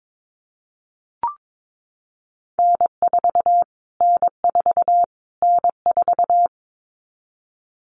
Morse code, heard as a keyed tone, about 22 wpm, 700 Hz. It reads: N4 N4 N4
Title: Callsign Prefixes for Top 25 DXCC - Repeated 3x - 22wpm